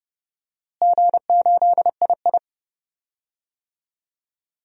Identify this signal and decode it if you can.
Morse code, heard as a keyed tone, about 32 wpm, 700 Hz. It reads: G8II
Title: DXCC-100 Call Signs - 4 Characters - Repeated 2x - 30wpm